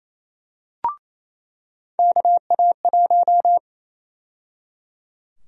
Morse code, heard as a keyed tone, about 28 wpm, 700 Hz. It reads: KA1
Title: USA Call Sign Prefixes - 28wpm